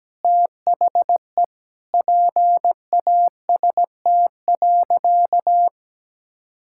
Morse code, heard as a keyed tone, about 17 wpm, 700 Hz. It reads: THE PAST.
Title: Sentences from Top 1000 Words - Review Part 6 - 17wpm